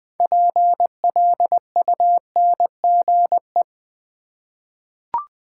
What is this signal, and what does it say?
Morse code, heard as a keyed tone, about 20 wpm, 700 Hz. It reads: PLUNGE
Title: 6 Letter Words - First 3 Letters Given - 20wpm